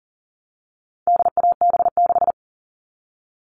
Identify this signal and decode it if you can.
Morse code, heard as a keyed tone, about 40 wpm, 700 Hz. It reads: DAB6
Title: All Cut Numbers - 4-Digit 40wpm